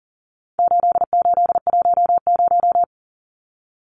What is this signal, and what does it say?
Morse code, heard as a keyed tone, about 40 wpm, 700 Hz. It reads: 8810
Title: Numbers - 4-Digit 40wpm